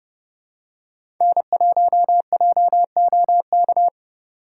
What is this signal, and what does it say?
Morse code, heard as a keyed tone, about 30 wpm, 700 Hz. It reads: N1JOK
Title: USA Call Signs - 5 Characters - 6x Character Spacing - 30wpm